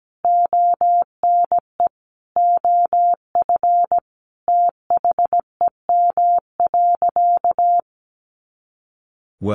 Morse code, heard as a keyed tone, about 17 wpm, 700 Hz. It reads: ONE OF THEM.
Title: Sentences from Top 300 Words 17wpm